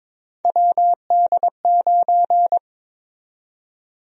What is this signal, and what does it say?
Morse code, heard as a keyed tone, about 22 wpm, 700 Hz. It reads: WD9
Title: USA Call Sign Prefixes - 22wpm